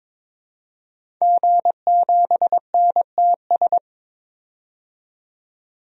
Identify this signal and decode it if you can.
Morse code, heard as a keyed tone, about 22 wpm, 700 Hz. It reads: G7NTS
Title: DXCC-100 Call Signs - 5 Characters - 22wpm